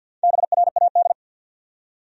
Morse code, heard as a keyed tone, about 48 wpm, 700 Hz. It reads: BRAD
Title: USA Common Names - Max 4 Letters - 50wpm